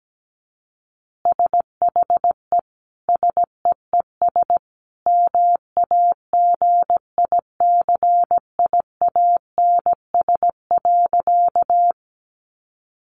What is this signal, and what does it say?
Morse code, heard as a keyed tone, about 17 wpm, 700 Hz. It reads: SHE SEES MAGICIANS.
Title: Easy Phrases 17wpm